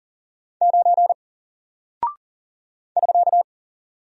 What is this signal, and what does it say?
Morse code, heard as a keyed tone, about 40 wpm, 700 Hz. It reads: 9 <SK>
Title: Single Character (Letter:Number:Punctuation:Prosign) Alt 40wpm